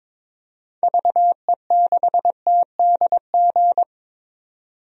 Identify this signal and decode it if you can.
Morse code, heard as a keyed tone, about 22 wpm, 700 Hz. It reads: VE6TDG
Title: DXCC-100 Call Signs - 6 Characters - 22wpm